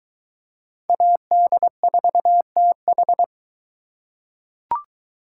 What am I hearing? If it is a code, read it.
Morse code, heard as a keyed tone, about 22 wpm, 700 Hz. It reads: AD4TH